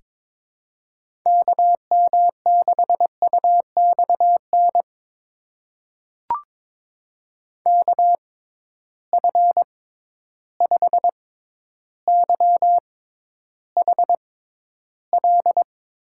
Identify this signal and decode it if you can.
Morse code, heard as a keyed tone, about 22 wpm, spaced as for 11 wpm, 700 Hz. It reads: KM6UXN KF5YHL
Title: USA Call Signs - 6 Characters - 6x Character Spacing - 22wpm